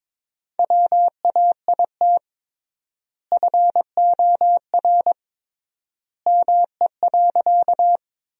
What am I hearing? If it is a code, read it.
Morse code, heard as a keyed tone, about 22 wpm, 700 Hz. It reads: WAIT FOR ME.
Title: 3x Word-Spacing - Sentences from Top 500 Words - 22wpm